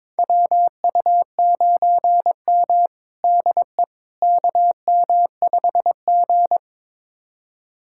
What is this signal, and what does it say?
Morse code, heard as a keyed tone, about 22 wpm, 700 Hz. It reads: WU9M DE KM5G
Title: USA Callsign DE Callsign - 4 Characters - 22wpm